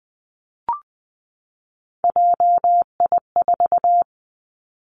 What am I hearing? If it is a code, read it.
Morse code, heard as a keyed tone, about 20 wpm, 700 Hz. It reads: JI4
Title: Three Challenging Letter-Number Combinations 20wpm